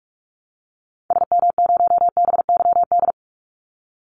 Morse code, heard as a keyed tone, about 46 wpm, 700 Hz. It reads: SM0BYD
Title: Call Signs - 45wpm